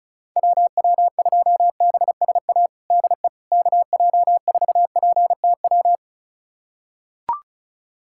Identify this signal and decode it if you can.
Morse code, heard as a keyed tone, about 36 wpm, 700 Hz. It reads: WW2BSA DE KJ4PTW
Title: USA Callsign DE Callsign - 6 Characters - 35wpm